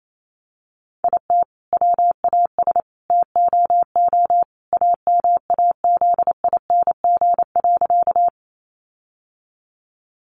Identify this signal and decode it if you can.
Morse code, heard as a keyed tone, about 28 wpm, 700 Hz. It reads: IT WAS TOO AMAZING.
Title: Sentences from Top 1500 Words 28wpm